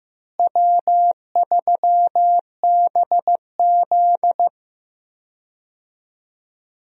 Morse code, heard as a keyed tone, about 15 wpm, 700 Hz. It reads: W3BZ
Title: USA Call Signs - 4 Characters - Repeated 3x - 15wpm